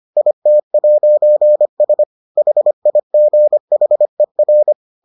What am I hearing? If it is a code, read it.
Morse code, heard as a keyed tone, about 25 wpm, 600 Hz. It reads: IT'S HIGHER